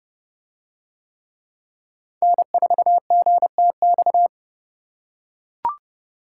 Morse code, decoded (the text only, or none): N4GTX